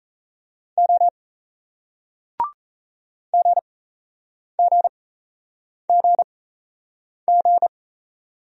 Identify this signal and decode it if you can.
Morse code, heard as a keyed tone, about 38 wpm, spaced as for 8 wpm, 700 Hz. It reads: O GGGG